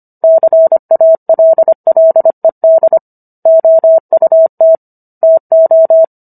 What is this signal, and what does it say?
Morse code, heard as a keyed tone, about 25 wpm, 650 Hz. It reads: CALLED OUT TO
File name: Book One / 16-Chapter 16 - 25 WPM